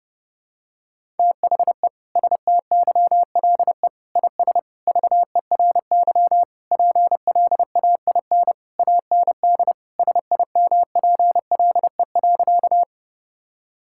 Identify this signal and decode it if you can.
Morse code, heard as a keyed tone, about 30 wpm, 700 Hz. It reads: THE STYLE IS VERY PLAIN AND SIMPLE.
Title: Sentences from Top 1000 Words - Review Part 5 - 30wpm